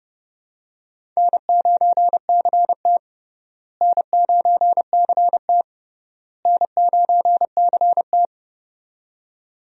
Morse code, heard as a keyed tone, about 30 wpm, 700 Hz. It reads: N9CT N9CT N9CT
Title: USA Call Signs - 4 Characters - Repeated 3x - 30wpm